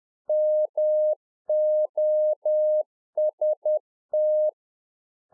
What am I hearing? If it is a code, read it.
Morse code, heard as a keyed tone, about 10 wpm, 610 Hz. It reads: MOST
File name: 2011-04-mp3 / RICK JJ0MPI 10WPM